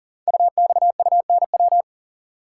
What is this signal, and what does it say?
Morse code, heard as a keyed tone, about 38 wpm, 700 Hz. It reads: UXUNW
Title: Code Groups - 5 Letters - 40wpm